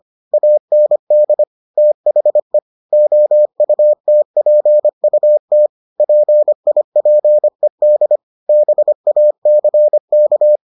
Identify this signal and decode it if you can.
Morse code, heard as a keyed tone, about 25 wpm, 600 Hz. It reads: AND THE OUTPUT PIPED BACK